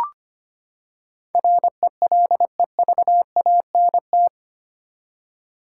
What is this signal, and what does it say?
Morse code, heard as a keyed tone, about 25 wpm, 700 Hz. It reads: RELEVANT
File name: Eight Letter Words - 25wpm